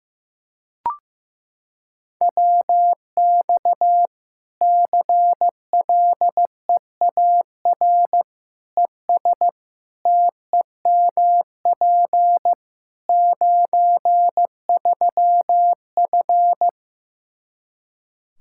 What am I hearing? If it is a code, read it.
Morse code, heard as a keyed tone, about 15 wpm, 700 Hz. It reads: WX CLEAR ES TEMP 93F
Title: K1USN SST QSO Practice 15wpm